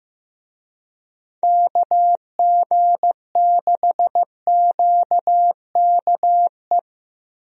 Morse code, heard as a keyed tone, about 15 wpm, 700 Hz. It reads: KG6QKE
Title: USA Call Signs - 6 Characters - Repeated 2x - 15wpm